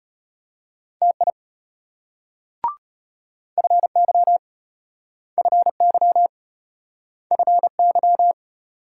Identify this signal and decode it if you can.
Morse code, heard as a keyed tone, about 34 wpm, 700 Hz. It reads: TI FY FY FY